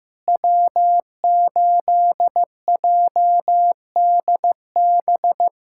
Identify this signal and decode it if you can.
Morse code, heard as a keyed tone, about 15 wpm, 700 Hz. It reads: W8JDB
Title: USA Callsign DE Callsign 15wpm